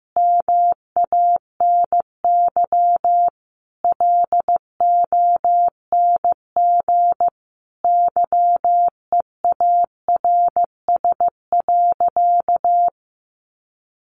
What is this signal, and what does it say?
Morse code, heard as a keyed tone, about 15 wpm, 700 Hz. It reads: MANY LONG YEARS.
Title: Sentences from Top 200 Words 15wpm